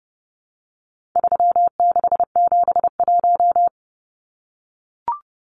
Morse code, heard as a keyed tone, about 30 wpm, 700 Hz. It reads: 3671
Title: Numbers - 4-Digit 30wpm